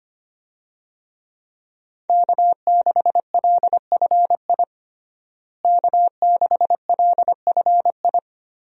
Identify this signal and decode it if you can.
Morse code, heard as a keyed tone, about 25 wpm, 700 Hz. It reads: K6LFI K6LFI
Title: USA Call Signs - 5 Characters - Repeated 3x - 25wpm